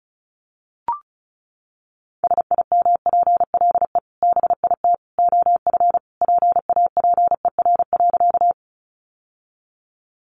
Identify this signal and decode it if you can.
Morse code, heard as a keyed tone, about 34 wpm, 700 Hz. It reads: SIMPLE BIT OF PAPER.
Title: Sentences from Top 900 Words 35wpm